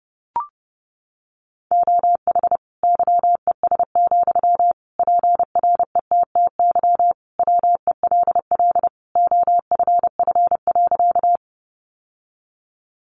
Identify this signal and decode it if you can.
Morse code, heard as a keyed tone, about 30 wpm, 700 Hz. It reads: OH YES, PRETTY WELL OFF.
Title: Sentences from Top 1000 Words 30wpm